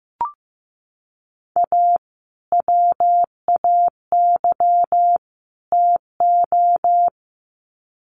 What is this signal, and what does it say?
Morse code, heard as a keyed tone, about 15 wpm, 700 Hz. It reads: A WAY TO
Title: Sets of 3 Words 15wpm